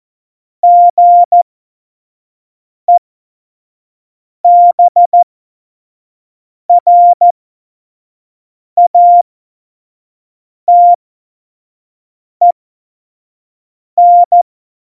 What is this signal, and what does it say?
Morse code, heard as a keyed tone, about 14 wpm, spaced as for 5 wpm, 700 Hz. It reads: GEBRATEN
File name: Morsekurs / Witze / Witz 14. Januar 2020, 25 F70